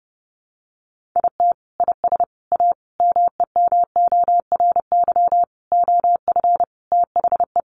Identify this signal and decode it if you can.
Morse code, heard as a keyed tone, about 30 wpm, 700 Hz. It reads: IT IS A MEMORY OF THE
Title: Sentences from Top 1300 Words 30wpm